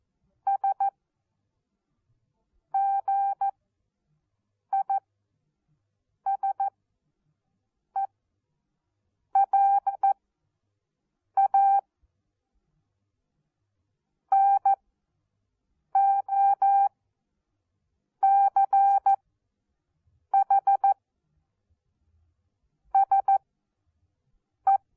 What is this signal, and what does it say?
Morse code, heard as a keyed tone, about 14 wpm, 790 Hz. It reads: S G I S E AE A N O C H S E
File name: Onlinelehrgang / morsen / DJ4UF Morsen14